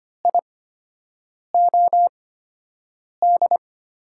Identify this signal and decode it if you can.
Morse code, heard as a keyed tone, about 25 wpm, spaced as for 7 wpm, 700 Hz. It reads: IOD
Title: USA Call Signs - 5 Characters - 8x Character Spacing - 25wpm